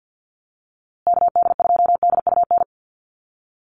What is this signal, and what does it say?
Morse code, heard as a keyed tone, about 50 wpm, 700 Hz. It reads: KD2DUN
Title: USA Call Signs Repeated 2x 50wpm